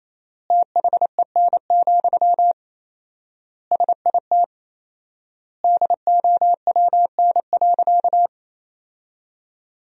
Morse code, heard as a keyed tone, about 28 wpm, 700 Hz. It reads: THEN, SIT DOWN.
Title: 4x Word-Spacing - Sentences from Top 500 Words - 28wpm